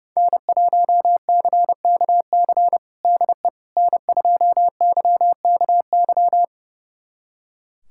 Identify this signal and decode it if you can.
Morse code, heard as a keyed tone, about 30 wpm, 700 Hz. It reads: N1CKC DE N2YKY
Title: USA Callsign DE Callsign - 5 Characters - 30wpm